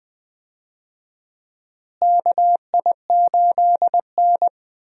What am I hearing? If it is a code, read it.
Morse code, heard as a keyed tone, about 21 wpm, 700 Hz. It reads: KI8N
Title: USA Call Signs - 4 Characters - Repeated 3x - 20wpm